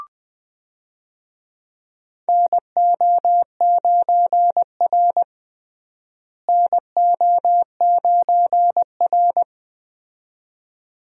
Morse code, heard as a keyed tone, about 20 wpm, 700 Hz. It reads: NO9R NO9R
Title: USA Call Signs - 4 Characters - Repeated 2x - 20wpm